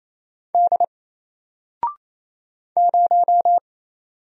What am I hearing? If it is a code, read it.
Morse code, heard as a keyed tone, about 27 wpm, 700 Hz. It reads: D 0
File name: Single-Letter-Number-28wpm